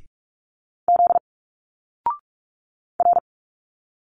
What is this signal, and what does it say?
Morse code, heard as a keyed tone, about 46 wpm, 700 Hz. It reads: Z R
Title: Single Letters 45wpm